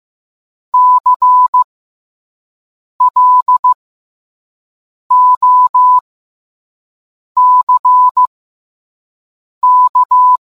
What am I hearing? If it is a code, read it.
Morse code, heard as a keyed tone, about 15 wpm, spaced as for 5 wpm, 1000 Hz. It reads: CLOCK